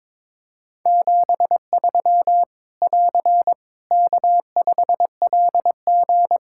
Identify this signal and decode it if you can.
Morse code, heard as a keyed tone, about 21 wpm, 700 Hz. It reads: 73 <AR> K5LG
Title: K1USN SST QSO Practice 22wpm